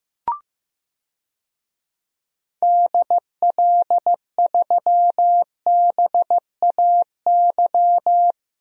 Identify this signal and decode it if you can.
Morse code, heard as a keyed tone, about 15 wpm, 700 Hz. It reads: DL3BAY
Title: DXCC-100 Call Signs - 6 Characters - Repeated 3x - 15wpm